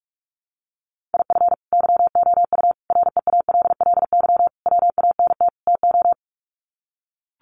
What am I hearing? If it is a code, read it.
Morse code, heard as a keyed tone, about 44 wpm, 700 Hz. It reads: IF YOU REALLY WANT TO
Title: Sets of 5 Words Encore2 45wpm